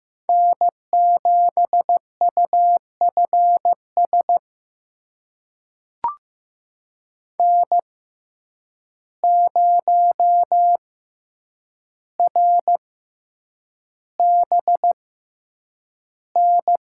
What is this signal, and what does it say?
Morse code, heard as a keyed tone, about 15 wpm, spaced as for 8 wpm, 700 Hz. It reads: N7UFS N0RBN
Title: USA Call Signs - 5 Characters - 6x Character Spacing - 15wpm